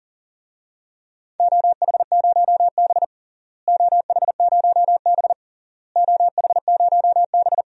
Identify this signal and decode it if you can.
Morse code, heard as a keyed tone, about 40 wpm, 700 Hz. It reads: OH0B OH0B OH0B
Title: DXCC-100 Call Signs - 4 Characters - Repeated 3x - 40wpm